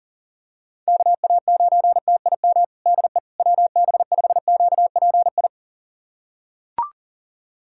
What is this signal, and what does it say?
Morse code, heard as a keyed tone, about 40 wpm, 700 Hz. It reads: KA9TIM DE WB5QPI
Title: USA Callsign DE Callsign - 6 Characters - 40wpm